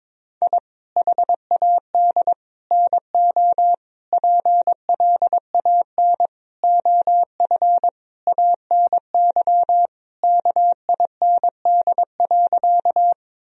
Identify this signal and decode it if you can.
Morse code, heard as a keyed tone, about 22 wpm, 700 Hz. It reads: I HAD NO PLAN OF ANY KIND.
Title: Sentences from Top 1000 Words - Review Part 2 - 22wpm